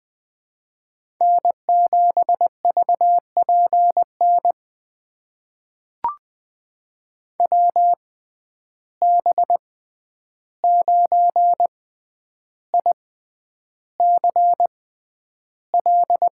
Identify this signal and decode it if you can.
Morse code, heard as a keyed tone, about 20 wpm, spaced as for 10 wpm, 700 Hz. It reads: N7VPN WB9ICL
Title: USA Call Signs - 6x Character Spacing - 20wpm